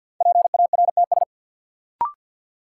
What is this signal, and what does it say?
Morse code, heard as a keyed tone, about 52 wpm, 700 Hz. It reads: PARTS
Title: Sentences from Top 1000 Words - Review Part 6 - New Word Drill - 50wpm